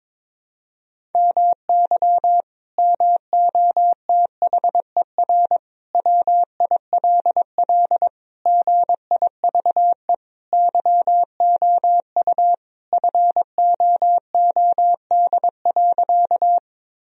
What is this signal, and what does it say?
Morse code, heard as a keyed tone, about 22 wpm, 700 Hz. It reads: MY MOTHER WILL GIVE YOU FOOD.
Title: Sentences from Top 1000 Words - Review Part 1 - 22wpm